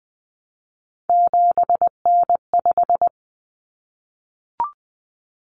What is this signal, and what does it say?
Morse code, heard as a keyed tone, about 20 wpm, 700 Hz. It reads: 7N5
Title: Common Cut Numbers - 3-Digit 20wpm